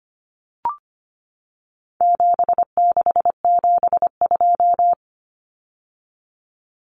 Morse code, heard as a keyed tone, about 25 wpm, 700 Hz. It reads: 7672
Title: Common Cut Numbers - 4-Digit 25wpm